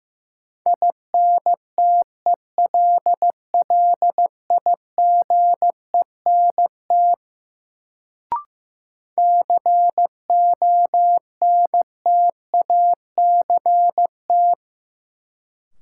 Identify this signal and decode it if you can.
Morse code, heard as a keyed tone, about 15 wpm, 700 Hz. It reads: INTELLIGENT CONTACT
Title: Sentences from Top 1000 Words - Review Part 7 - New Word Drill - 15wpm